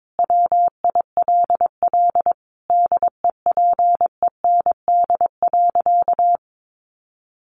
Morse code, heard as a keyed tone, about 22 wpm, 700 Hz. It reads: WILL DEPEND.
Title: Sentences from Top 1300 Words 22wpm